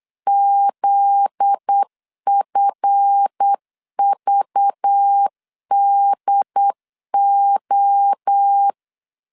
Morse code, heard as a keyed tone, about 8 wpm, 800 Hz. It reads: ZFVDO
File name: LEZIONI / 7 GF9Z / 7 GF9Z 8wpm 3213